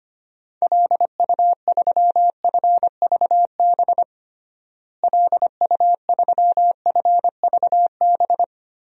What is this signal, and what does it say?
Morse code, heard as a keyed tone, about 25 wpm, 700 Hz. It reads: LU3FVB LU3FVB
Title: DXCC-100 Call Signs - 6 Characters - Repeated 3x - 25wpm